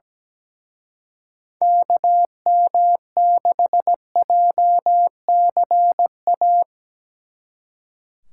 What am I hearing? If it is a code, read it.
Morse code, heard as a keyed tone, about 19 wpm, 700 Hz. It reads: KM6JCA
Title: Speed Racing - USA Call Signs 17wpm